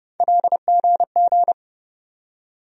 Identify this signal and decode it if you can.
Morse code, heard as a keyed tone, about 30 wpm, 700 Hz. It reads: LGG
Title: USA Call Sign Postfixes - 3 Letters 30wpm